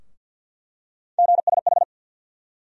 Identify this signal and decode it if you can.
Morse code, heard as a keyed tone, about 47 wpm, 700 Hz. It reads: GSH
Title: USA Call Sign Postfixes - 3 Letters 50wpm